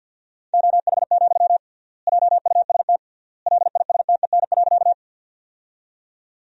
Morse code, heard as a keyed tone, about 50 wpm, 700 Hz. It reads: OH, JUST LISTEN.
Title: 3x Word-Spacing - Sentences from Top 500 Words - 50wpm